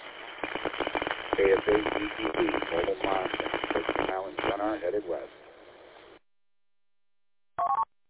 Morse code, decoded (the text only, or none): none